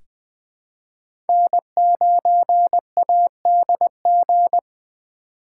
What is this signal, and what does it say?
Morse code, heard as a keyed tone, about 20 wpm, 700 Hz. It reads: N9ADG
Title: K1USN SST QSO Practice - 20wpm